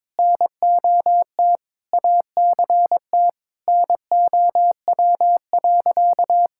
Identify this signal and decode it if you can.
Morse code, heard as a keyed tone, about 22 wpm, 700 Hz. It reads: NOT ACT NOW.
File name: practice / 22wpm / Sentences with 8-Word Phrases - 22wpm